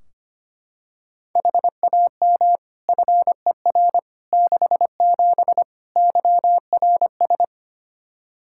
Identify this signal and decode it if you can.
Morse code, heard as a keyed tone, about 25 wpm, 700 Hz. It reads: HAM FER 67 YRS
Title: Practice QSO - Exchange 3 - More Info - 25wpm